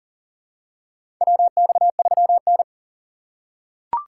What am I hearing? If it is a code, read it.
Morse code, heard as a keyed tone, about 40 wpm, 700 Hz. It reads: WX3N
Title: USA Call Signs - 4 Characters - 4x Character Spacing - 40wpm